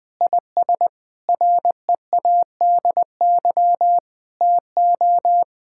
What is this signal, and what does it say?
Morse code, heard as a keyed tone, about 20 wpm, 700 Hz. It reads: IS READY TO
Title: Sentences from Top 1600 Words 20wpm